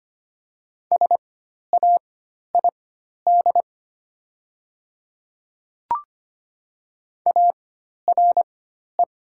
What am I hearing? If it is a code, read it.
Morse code, heard as a keyed tone, about 25 wpm, spaced as for 12 wpm, 700 Hz. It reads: SAID ARE
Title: 4x Character Spacing - Top 100 Words - 25wpm